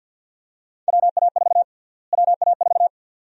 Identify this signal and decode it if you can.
Morse code, heard as a keyed tone, about 51 wpm, 700 Hz. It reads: WA4 WA4
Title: USA Call Sign Prefixes - Repeated 2x - 50wpm